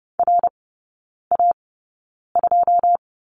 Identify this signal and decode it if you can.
Morse code, heard as a keyed tone, about 30 wpm, spaced as for 9 wpm, 700 Hz. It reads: RA2
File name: course / 30:10wpm / 087 Lesson 15e - Number 5 - Three Characters 30wpm